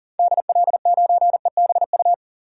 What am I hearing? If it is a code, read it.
Morse code, heard as a keyed tone, about 38 wpm, 700 Hz. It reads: DL9EBU
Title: DXCC-100 Call Signs - 6 Characters - 40wpm